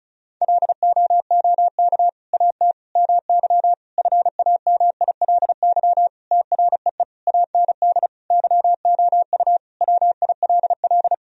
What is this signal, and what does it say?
Morse code, heard as a keyed tone, about 36 wpm, 700 Hz. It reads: LOOK AT MY FAMILY TREE AND YOU WILL
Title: Sentences from Top 1000 Words - Review Part 1 - 35wpm